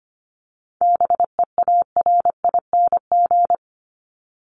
Morse code, heard as a keyed tone, about 25 wpm, 700 Hz. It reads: BEARING
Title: ING Suffix - Top 100 Words with Base Words 25wpm